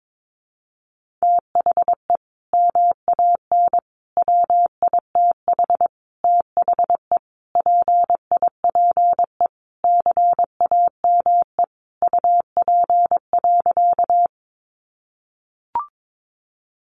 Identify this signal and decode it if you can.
Morse code, heard as a keyed tone, about 22 wpm, 700 Hz. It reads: THE MAN WITH THE PIPE CAME UP.
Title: Sentences from Top 1500 Words 22wpm